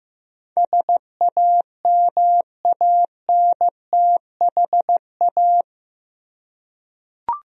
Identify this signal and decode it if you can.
Morse code, heard as a keyed tone, about 15 wpm, 700 Hz. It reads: SAMANTHA
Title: USA Common Names - 15wpm